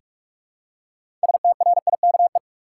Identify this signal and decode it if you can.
Morse code, heard as a keyed tone, about 43 wpm, 700 Hz. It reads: STRIKE
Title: Six Letter Words - 45wpm